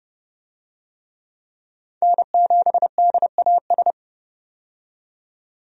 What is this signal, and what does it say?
Morse code, heard as a keyed tone, about 31 wpm, 700 Hz. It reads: N7DAS
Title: USA Call Signs - 5 Characters - 30wpm